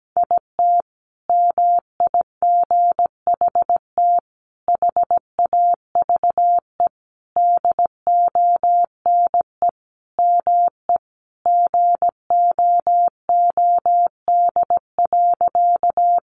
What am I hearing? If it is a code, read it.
Morse code, heard as a keyed tone, about 17 wpm, 700 Hz. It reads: IT MIGHT HAVE DONE ME GOOD.
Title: Sentences from Top 500 Words 17wpm